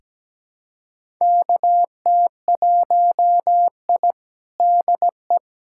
Morse code, heard as a keyed tone, about 17 wpm, 700 Hz. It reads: KT1I DE
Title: USA Callsign DE Callsign - 4 Characters - 17wpm